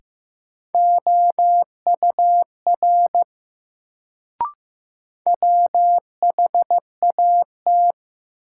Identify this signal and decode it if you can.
Morse code, heard as a keyed tone, about 15 wpm, 700 Hz. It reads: OUR WHAT
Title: Sets of 2 Words - New Word Drill - 15wpm